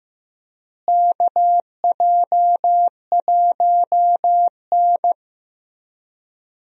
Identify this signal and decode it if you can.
Morse code, heard as a keyed tone, about 15 wpm, 700 Hz. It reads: KJ1N
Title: USA Call Signs - 4 Characters - Repeated 3x - 15wpm